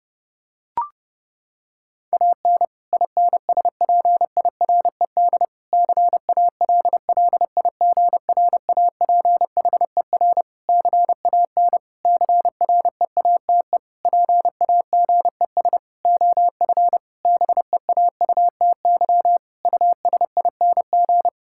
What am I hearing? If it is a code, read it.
Morse code, heard as a keyed tone, about 30 wpm, 700 Hz. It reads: AN INSPIRED CALLIGRAPHER CAN CREATE PAGES OF BEAUTY USING
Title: Pangram Challenge v2 - 30wpm